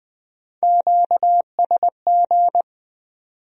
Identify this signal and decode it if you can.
Morse code, heard as a keyed tone, about 20 wpm, 700 Hz. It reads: QSG
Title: Speed-Racing Q-Codes - 17wpm